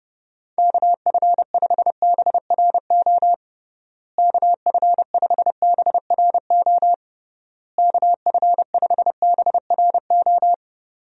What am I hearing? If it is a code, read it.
Morse code, heard as a keyed tone, about 30 wpm, 700 Hz. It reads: KF5BRO KF5BRO KF5BRO
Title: USA Call Signs - 6 Characters - Repeated 3x - 30wpm